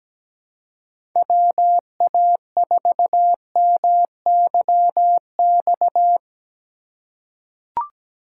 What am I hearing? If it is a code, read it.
Morse code, heard as a keyed tone, about 17 wpm, 700 Hz. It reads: WA4MYX
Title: USA Call Signs - 6 Characters - 3x Character Spacing - 17wpm